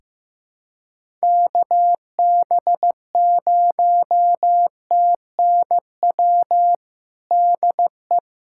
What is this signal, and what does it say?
Morse code, heard as a keyed tone, about 15 wpm, 700 Hz. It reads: KB0TNW DE
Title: USA Callsign DE Callsign - 6 Characters - 15wpm